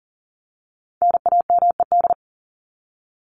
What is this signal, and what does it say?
Morse code, heard as a keyed tone, about 39 wpm, 700 Hz. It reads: NAMED
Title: ED Suffix - Top 100 Words with Base Words 40wpm